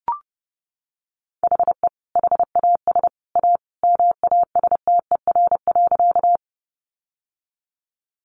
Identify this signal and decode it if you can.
Morse code, heard as a keyed tone, about 30 wpm, 700 Hz. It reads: HE HAS A MASTER.
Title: Sentences from Top 1300 Words 30wpm